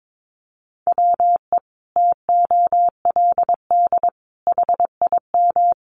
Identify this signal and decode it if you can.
Morse code, heard as a keyed tone, about 22 wpm, 700 Hz. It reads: WE TOLD HIM
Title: Sentences from Top 1300 Words 22wpm